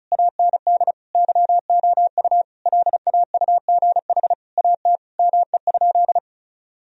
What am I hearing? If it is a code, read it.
Morse code, heard as a keyed tone, about 34 wpm, 700 Hz. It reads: AND YOU LAUGH AT ME?
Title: Sentences from Top 1000 Words - Review Part 2 - 35wpm